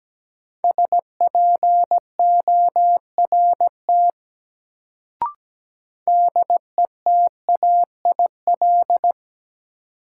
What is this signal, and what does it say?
Morse code, heard as a keyed tone, about 17 wpm, 700 Hz. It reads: SPORT DETAIL
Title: Sentences from Top 1500 Words - New Word Drill Alt - 17wpm